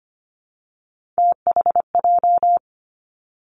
Morse code, heard as a keyed tone, about 25 wpm, 700 Hz. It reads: THJ
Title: Three Challenging Letter-Number Combinations 25wpm